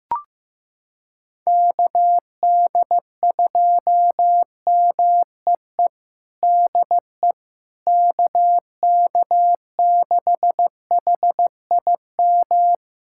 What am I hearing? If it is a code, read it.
Morse code, heard as a keyed tone, about 15 wpm, 700 Hz. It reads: KD2MEE DE KK6HIM
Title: USA Callsign DE Callsign - 6 Characters - 15wpm